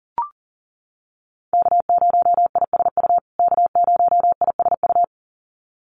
Morse code, heard as a keyed tone, about 40 wpm, 700 Hz. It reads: K0ISU K0ISU
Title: USA Call Signs Repeated 2x 40wpm